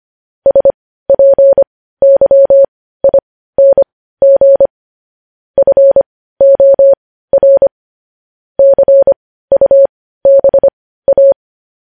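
Morse code, heard as a keyed tone, about 25 wpm, spaced as for 15 wpm, 550 Hz. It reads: SPYING FOR CUBA
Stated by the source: Morse code news headlines at 15 wpm.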